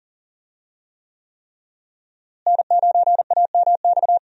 Morse code, heard as a keyed tone, about 40 wpm, 690 Hz. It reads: N9AMX
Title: USA Call Signs - 5 Characters - Repeated 3x - 40wpm